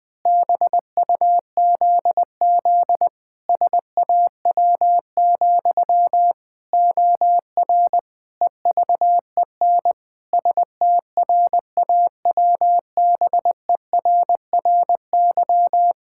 Morse code, heard as a keyed tone, about 20 wpm, 700 Hz. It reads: BUZZ SAW, OR EVEN STRAWBERRY
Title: Pangram Challenge v2 - 20wpm